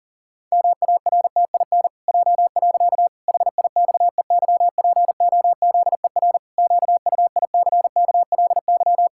Morse code, heard as a keyed tone, about 37 wpm, 700 Hz. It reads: MARTIN J. HIXEYPOZER QUICKLY
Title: Pangram Challenge v2 - 40wpm